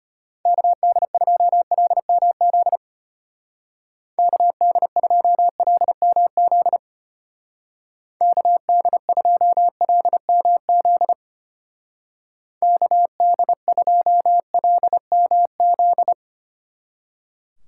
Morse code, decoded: KD2LMZ KD2LMZ KD2LMZ KD2LMZ

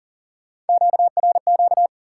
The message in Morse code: QRQ

